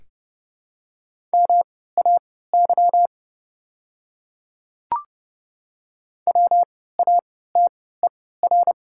Morse code, heard as a keyed tone, about 30 wpm, spaced as for 17 wpm, 700 Hz. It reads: MAY WATER
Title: 3x Character Spacing - Top 100 Words - 30wpm